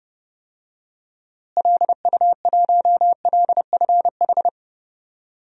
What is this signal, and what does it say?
Morse code, heard as a keyed tone, about 30 wpm, 700 Hz. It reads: LU1LFH
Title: DXCC-100 Call Signs - 6 Characters - Repeated 3x - 30wpm